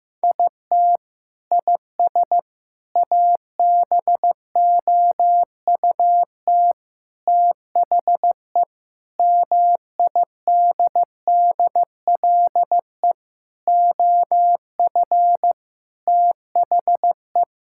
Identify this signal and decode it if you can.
Morse code, heard as a keyed tone, about 16 wpm, 700 Hz. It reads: IT IS ABOUT THE MIDDLE OF THE
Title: Sentences from Top 1000 Words - Review Part 6 - 15wpm